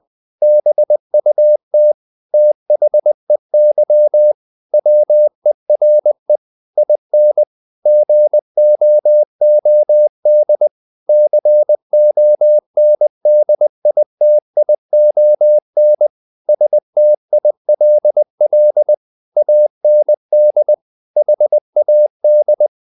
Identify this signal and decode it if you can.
Morse code, heard as a keyed tone, about 20 wpm, 600 Hz. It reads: BUT THEY WERE IN GOOD CONDITION STILL AND HAD
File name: the old man and the sea / the old man and the sea part 1 20wpm